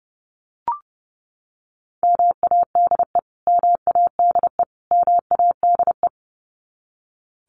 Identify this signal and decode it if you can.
Morse code, heard as a keyed tone, about 30 wpm, 700 Hz. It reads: MADE MADE MADE
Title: Top 300 Words Repeated 30wpm